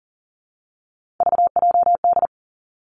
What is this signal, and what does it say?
Morse code, heard as a keyed tone, about 42 wpm, 700 Hz. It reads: VJD
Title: Three Challenging Letter-Number Combinations 40wpm